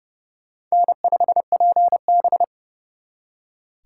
Morse code, heard as a keyed tone, about 37 wpm, 700 Hz. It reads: N5PB
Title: Speed Racing - USA 1x2 Call Signs 30wpm